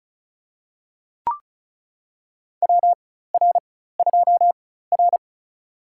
Morse code, heard as a keyed tone, about 36 wpm, spaced as for 16 wpm, 700 Hz. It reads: WR2R
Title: USA Call Signs - 4 Characters - 4x Character Spacing - 35wpm